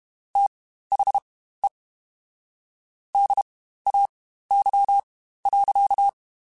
Morse code, heard as a keyed tone, about 32 wpm, spaced as for 15 wpm, 800 Hz. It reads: THE DAY.
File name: Spaced Characters / Spaced Characters-mp3s / Weather 5 12 sp @30